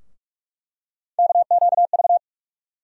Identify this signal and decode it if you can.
Morse code, heard as a keyed tone, about 46 wpm, 700 Hz. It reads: KQV
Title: USA Call Sign Postfixes - 3 Letters 45wpm